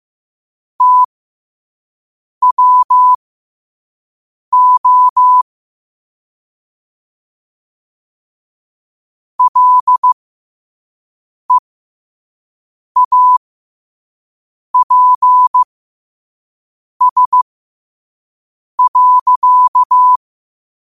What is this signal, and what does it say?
Morse code, heard as a keyed tone, about 15 wpm, spaced as for 5 wpm, 1000 Hz. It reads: TWO LEAPS.
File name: qotdfiles / 5wpm